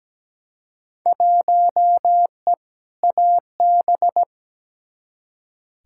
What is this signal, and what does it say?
Morse code, heard as a keyed tone, about 17 wpm, 700 Hz. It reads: 1E AB
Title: Field Day Exchanges - 17wpm